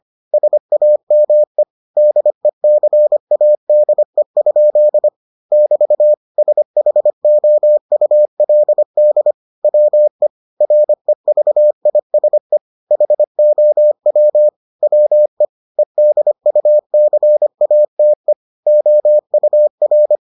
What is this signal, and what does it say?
Morse code, heard as a keyed tone, about 25 wpm, 600 Hz. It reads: SAME DECADE? <BT> SHOULD WE REVISE HOW WE EDUCATE OUR